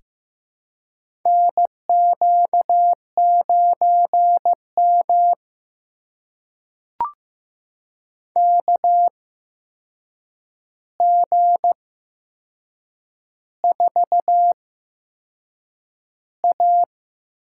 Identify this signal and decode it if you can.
Morse code, heard as a keyed tone, about 15 wpm, spaced as for 6 wpm, 700 Hz. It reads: NQ9M KG4A
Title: USA Call Signs - 4 Characters - 8x Character Spacing - 15wpm